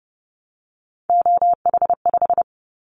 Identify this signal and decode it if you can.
Morse code, heard as a keyed tone, about 30 wpm, 700 Hz. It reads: OH5
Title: Callsign Prefixes for Top 25 DXCC - 30wpm